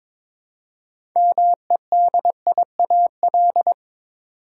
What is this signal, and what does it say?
Morse code, heard as a keyed tone, about 22 wpm, 700 Hz. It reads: MEDIAL
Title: Practice QSO - Exchange 3 - More Info - 22wpm